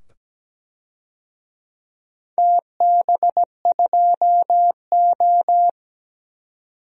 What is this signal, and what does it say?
Morse code, heard as a keyed tone, about 17 wpm, 700 Hz. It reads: TB2O
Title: DXCC-100 Call Signs - 4 Characters - 17wpm